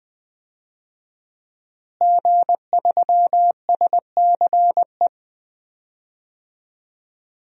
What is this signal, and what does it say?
Morse code, heard as a keyed tone, about 20 wpm, 700 Hz. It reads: G3SCE
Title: DXCC-100 Call Signs - 5 Characters - Repeated 2x - 20wpm